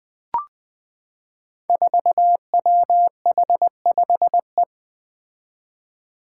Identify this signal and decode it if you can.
Morse code, heard as a keyed tone, about 20 wpm, 700 Hz. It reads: 4WH5E
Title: Five Challenging Letter-Number Combinations 20wpm